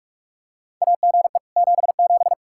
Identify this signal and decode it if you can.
Morse code, heard as a keyed tone, about 46 wpm, 700 Hz. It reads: AGE 77
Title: Practice QSO - Exchange 3 - More Info - 45wpm